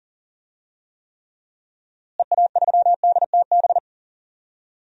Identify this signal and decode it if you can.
Morse code, heard as a keyed tone, about 40 wpm, 700 Hz. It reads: EA3DTB
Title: DXCC-100 Call Signs - 6 Characters - Repeated 3x - 40wpm